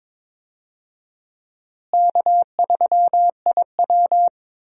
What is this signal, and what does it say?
Morse code, heard as a keyed tone, about 22 wpm, 700 Hz. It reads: K3IW
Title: USA Call Signs - 4 Characters - Repeated 3x - 22wpm